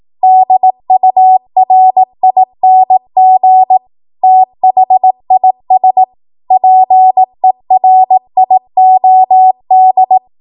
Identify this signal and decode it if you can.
Morse code, heard as a keyed tone, about 18 wpm, 750 Hz. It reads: DURING THIS PERIOD
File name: Archive / 18 WPM / 18WPM